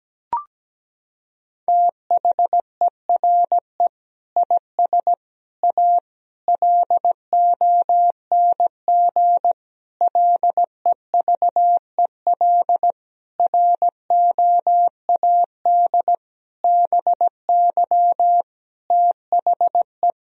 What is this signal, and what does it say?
Morse code, heard as a keyed tone, about 17 wpm, 700 Hz. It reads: THERE IS A LONG LEVEL ROAD BY THE
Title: Sentences from Top 1000 Words - Review Part 4 - 17wpm